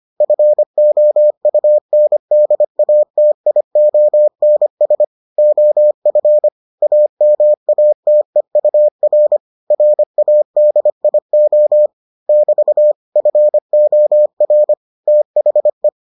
Morse code, heard as a keyed tone, about 25 wpm, 600 Hz. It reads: FOUNDATIONS OF AMATEUR RADIO <BT> FOR THE